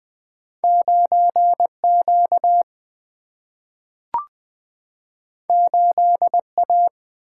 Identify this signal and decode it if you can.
Morse code, heard as a keyed tone, about 20 wpm, 700 Hz. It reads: 9Q 8A
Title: Single-Letter Single-Number 20wpm